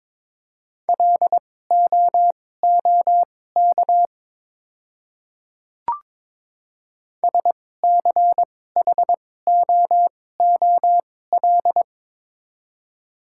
Morse code, heard as a keyed tone, about 22 wpm, spaced as for 16 wpm, 700 Hz. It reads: LOOK SCHOOL
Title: 2x Character Spacing - Top 300 Words - 22wpm